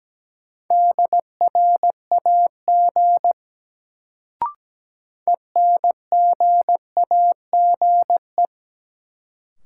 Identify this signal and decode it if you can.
Morse code, heard as a keyed tone, about 17 wpm, 700 Hz. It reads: DRAG ENGAGE
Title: Sentences from Top 1500 Words - New Word Drill - 17wpm